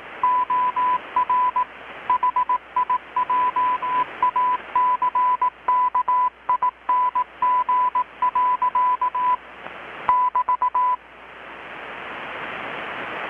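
Morse code, heard as a keyed tone, about 18 wpm, 1000 Hz. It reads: OR HIJACKING. <BT>
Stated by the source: Morse from KPH's Night of Nights XX maritime radiotelegraphy event, received in Chiapas, Mexico about 3,800 km away on a Kenwood R-600 with a grounded 10 m long-wire antenna.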